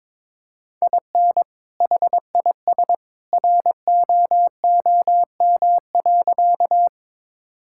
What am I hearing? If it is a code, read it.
Morse code, heard as a keyed tone, about 22 wpm, 700 Hz. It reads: IN HIS ROOM.
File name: practice / 22wpm / Sentences with 2-Word Phrases - 22wpm